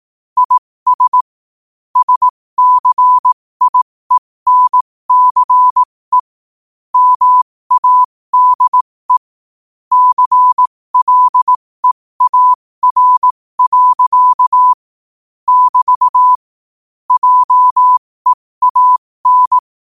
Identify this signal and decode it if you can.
Morse code, heard as a keyed tone, about 18 wpm, spaced as for 15 wpm, 1000 Hz. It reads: IS SCIENCE MADE CLEAR. <BT> JEAN
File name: qotdfiles / 15wpm